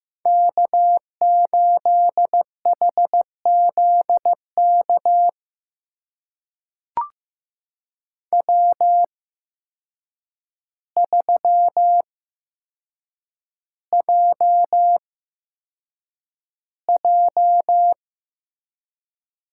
Morse code, heard as a keyed tone, about 15 wpm, spaced as for 7 wpm, 700 Hz. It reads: K8HZK W3JJ
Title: USA Call Signs - 5 Characters - 8x Character Spacing - 15wpm